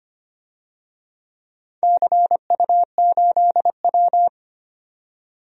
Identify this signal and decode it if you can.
Morse code, heard as a keyed tone, about 25 wpm, 700 Hz. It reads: CU8W
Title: DXCC-100 Call Signs - 4 Characters - 25wpm